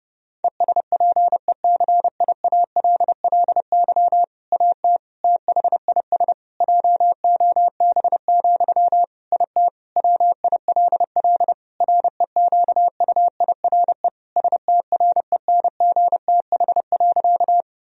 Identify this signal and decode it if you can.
Morse code, heard as a keyed tone, about 30 wpm, 700 Hz. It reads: ESPECIALLY AT THIS JOB, IT WILL REQUIRE STRENGTH.
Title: Sentences from Top 1000 Words - Review Part 4 - 30wpm